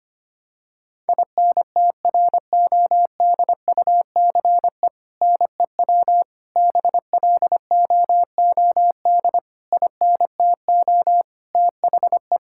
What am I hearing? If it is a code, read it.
Morse code, heard as a keyed tone, about 25 wpm, 700 Hz. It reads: INTRODUCE NEW BLOOD INTO THE
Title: Sentences from Top 1600 Words 25wpm